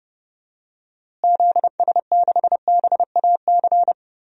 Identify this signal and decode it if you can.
Morse code, heard as a keyed tone, about 30 wpm, 700 Hz. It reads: ZS6BAC
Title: DXCC-100 Call Signs - 6 Characters - Repeated 3x - 30wpm